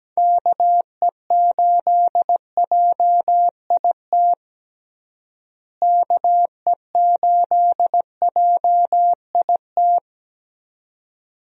Morse code, heard as a keyed tone, about 17 wpm, 700 Hz. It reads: KE8JIT KE8JIT
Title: USA Call Signs - 6 Characters - Repeated 2x - 17wpm